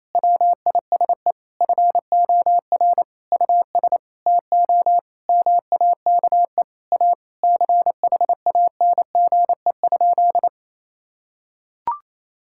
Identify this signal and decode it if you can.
Morse code, heard as a keyed tone, about 28 wpm, 700 Hz. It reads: WISE FOR US TO MAKE A CHANGE?
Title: Sentences from Top 1600 Words 28wpm